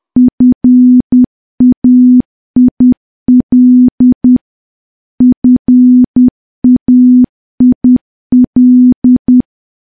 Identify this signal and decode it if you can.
Morse code, heard as a keyed tone, about 10 wpm, 250 Hz. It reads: FAIL FAIL
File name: morse / morse-training-10WPM